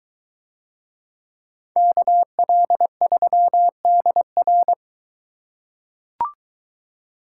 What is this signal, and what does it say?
Morse code, heard as a keyed tone, about 21 wpm, 700 Hz. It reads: KL3DR